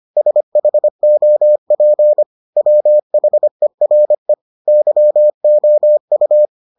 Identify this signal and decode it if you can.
Morse code, heard as a keyed tone, about 25 wpm, 600 Hz. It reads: SHOP WHERE YOU